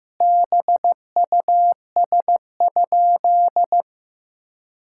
Morse code, heard as a keyed tone, about 15 wpm, 700 Hz. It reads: BUS?
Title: Sentences from Top 1000 Words - Review Part 6 - 15wpm